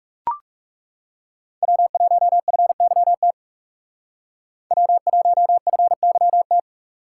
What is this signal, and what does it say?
Morse code, heard as a keyed tone, about 38 wpm, 700 Hz. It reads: W1FYT W1FYT